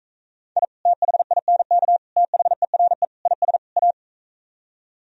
Morse code, heard as a keyed tone, about 45 wpm, 700 Hz. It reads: I THINK THERE IS A